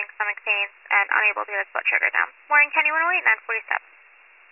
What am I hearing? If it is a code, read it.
Emergency Alert System (SAME) tones, heard as harsh buzzing bursts, with speech. It is heard from 0.7 to 3.6 s.